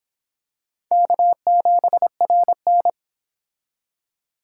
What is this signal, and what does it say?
Morse code, heard as a keyed tone, about 25 wpm, 700 Hz. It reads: K7RN